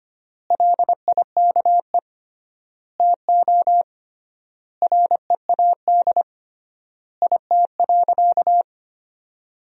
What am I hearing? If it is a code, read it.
Morse code, heard as a keyed tone, about 25 wpm, 700 Hz. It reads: LIKE TO READ IT.
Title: 3x Word-Spacing - Sentences from Top 500 Words - 25wpm